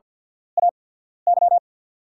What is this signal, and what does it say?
Morse code, heard as a keyed tone, about 48 wpm, spaced as for 13 wpm, 690 Hz. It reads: AY
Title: 8x Character Spacing - Top 500 Words - 50wpm